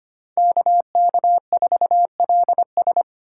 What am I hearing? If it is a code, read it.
Morse code, heard as a keyed tone, about 25 wpm, 700 Hz. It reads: KK4LS
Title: USA Call Signs - 5 Characters - 4x Character Spacing - 25wpm